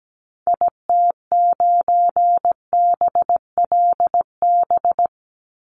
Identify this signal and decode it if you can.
Morse code, heard as a keyed tone, about 17 wpm, 700 Hz. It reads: IT9BLB
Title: Call Signs Repeated 3x - 17wpm